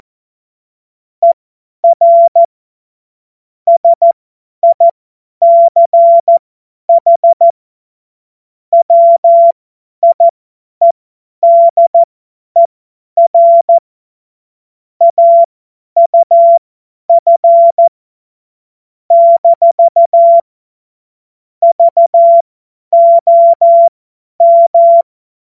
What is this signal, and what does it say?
Morse code, heard as a keyed tone, about 14 wpm, spaced as for 10 wpm, 680 Hz. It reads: ER SICH WIEDER AUF - VOM
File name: Morsekurs / Witze / Witz 14. Februar 2023 50 F70